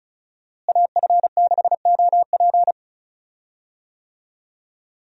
Morse code, heard as a keyed tone, about 36 wpm, 700 Hz. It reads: AF6OP
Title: USA Call Signs - 5 Characters - Repeated 2x - 35wpm